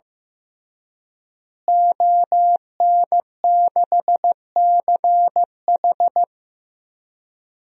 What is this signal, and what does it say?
Morse code, heard as a keyed tone, about 16 wpm, 700 Hz. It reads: ON6CH
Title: DXCC-100 Call Signs - 5 Characters - Repeated 3x - 15wpm